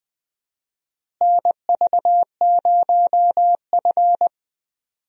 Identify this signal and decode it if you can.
Morse code, heard as a keyed tone, about 20 wpm, 700 Hz. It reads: NV0F
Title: USA Call Signs - 4 Characters - Repeated 3x - 20wpm